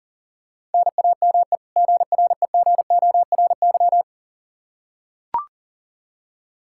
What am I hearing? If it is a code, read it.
Morse code, heard as a keyed tone, about 40 wpm, 700 Hz. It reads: NAME GREGORY
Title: Practice QSO - Exchange 1 - Beginning - 40wpm